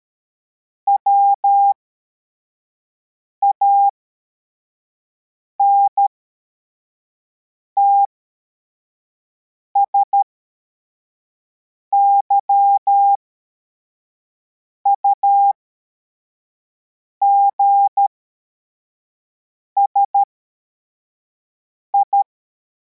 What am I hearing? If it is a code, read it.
Morse code, heard as a keyed tone, about 13 wpm, spaced as for 4 wpm, 800 Hz. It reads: WANTS YUGSI
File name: formation / cw / Text61